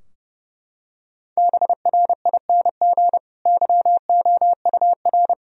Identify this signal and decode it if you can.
Morse code, heard as a keyed tone, about 30 wpm, 700 Hz. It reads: BRING YOUR
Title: Sentences from Top 1000 Words - Review Part 3 - 30wpm